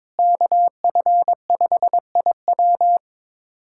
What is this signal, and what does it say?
Morse code, heard as a keyed tone, about 22 wpm, 700 Hz. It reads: KF5IW
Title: USA Callsign DE Callsign - 5 Characters - 22wpm